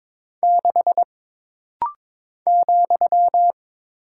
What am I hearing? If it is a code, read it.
Morse code, heard as a keyed tone, about 22 wpm, 700 Hz. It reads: 6 ,
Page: This practice set is composed of Letters, Numbers, common punctuations, and common prosigns.